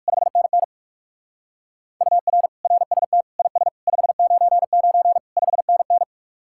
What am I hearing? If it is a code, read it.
Morse code, heard as a keyed tone, about 55 wpm, 700 Hz. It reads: 5NN UR RST IS 599 5NN
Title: Speed-Racing Common QSO Phrases - 45wpm